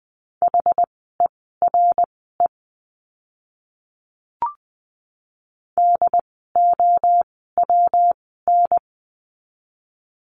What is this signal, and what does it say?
Morse code, heard as a keyed tone, about 20 wpm, spaced as for 15 wpm, 700 Hz. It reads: HERE DOWN
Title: 139 Lesson 23f - Letter f - Words 20wpm